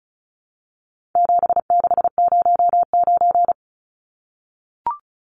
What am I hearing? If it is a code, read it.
Morse code, heard as a keyed tone, about 34 wpm, 700 Hz. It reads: 7609
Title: Numbers - 4-Digit 35wpm